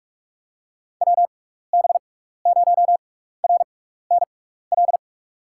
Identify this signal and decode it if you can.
Morse code, heard as a keyed tone, about 44 wpm, spaced as for 15 wpm, 700 Hz. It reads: WB0RNL
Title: USA Call Signs - 6 Characters - 6x Character Spacing - 45wpm